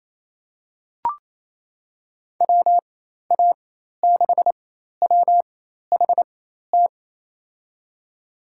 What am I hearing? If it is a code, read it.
Morse code, heard as a keyed tone, about 28 wpm, spaced as for 13 wpm, 700 Hz. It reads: WA6WHT
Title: USA Call Signs - 6 Characters - 4x Character Spacing - 28wpm